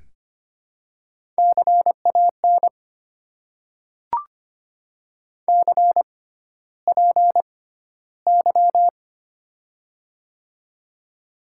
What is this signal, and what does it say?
Morse code, heard as a keyed tone, about 25 wpm, spaced as for 13 wpm, 700 Hz. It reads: CAN CPY
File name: Top 90 QSO Elements - 6x Character Spacing - 25wpm